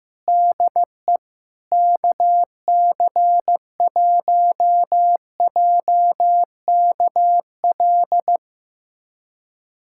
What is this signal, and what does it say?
Morse code, heard as a keyed tone, about 15 wpm, 700 Hz. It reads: DE KC1JKL
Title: USA Callsign DE Callsign 15wpm